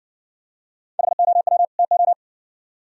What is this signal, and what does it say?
Morse code, heard as a keyed tone, about 60 wpm, 700 Hz. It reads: HOW TO